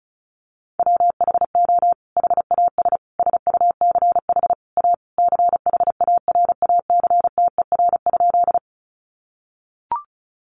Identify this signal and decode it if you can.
Morse code, heard as a keyed tone, about 34 wpm, 700 Hz. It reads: WHO HAS SUCH A CHARACTER?
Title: Sentences from Top 1200 Words 35wpm